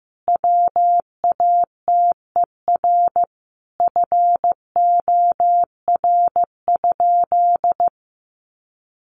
Morse code, heard as a keyed tone, about 15 wpm, 700 Hz. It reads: WATER FOR?
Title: Sentences from Top 1300 Words 15wpm